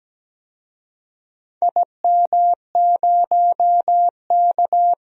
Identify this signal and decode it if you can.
Morse code, heard as a keyed tone, about 17 wpm, 700 Hz. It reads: IM0K
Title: DXCC-100 Call Signs - 4 Characters - 17wpm